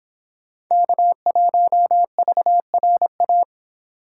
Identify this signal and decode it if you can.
Morse code, heard as a keyed tone, about 23 wpm, 700 Hz. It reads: K1VRA